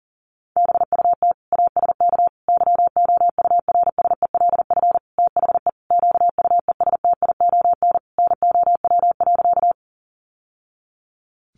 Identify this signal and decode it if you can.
Morse code, heard as a keyed tone, about 40 wpm, 700 Hz. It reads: BUT ASK YOURSELF THE QUESTION NOW.
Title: Sentences from Top 600 Words 40wpm